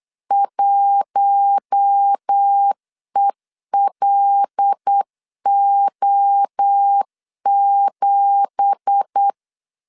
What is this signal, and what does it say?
Morse code, heard as a keyed tone, about 8 wpm, 800 Hz. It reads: 1ELO7
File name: LEZIONI / 9 2867 8wpm 3440